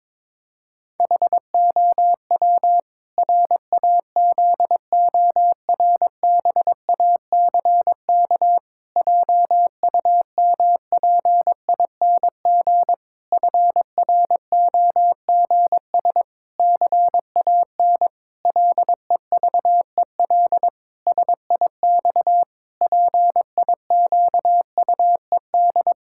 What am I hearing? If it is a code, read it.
Morse code, heard as a keyed tone, about 21 wpm, 700 Hz. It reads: HOW RAZORBACK JUMPING FROGS CAN LEVEL SIX PIQUED
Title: Pangram Challenge v2 - 22wpm